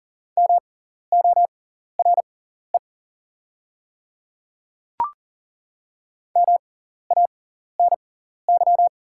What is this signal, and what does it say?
Morse code, heard as a keyed tone, about 40 wpm, spaced as for 14 wpm, 700 Hz. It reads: MORE MANY
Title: 6x Character Spacing - Top 100 Words - 40wpm